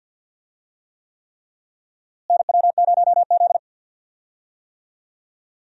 Morse code, heard as a keyed tone, about 49 wpm, 700 Hz. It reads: NW0Z
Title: USA Call Signs - 4 Characters - Repeated 3x - 50wpm